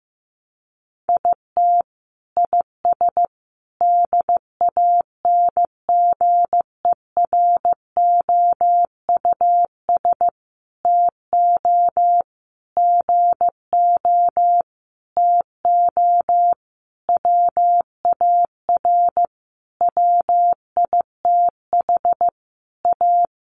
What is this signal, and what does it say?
Morse code, heard as a keyed tone, about 15 wpm, 700 Hz. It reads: IT IS DANGEROUS TO GO TO WAR WITH A
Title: Sentences from Top 1200 Words 15wpm